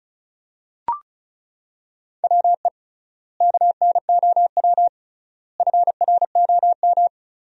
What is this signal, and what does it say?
Morse code, heard as a keyed tone, about 34 wpm, 700 Hz. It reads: WE KNOW FROM
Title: 3x Word-Spacing - Sentences from Top 500 Words - 35wpm